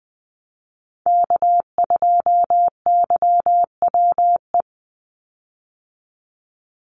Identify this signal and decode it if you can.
Morse code, heard as a keyed tone, about 20 wpm, 700 Hz. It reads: K2YWE
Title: Call Signs - 20wpm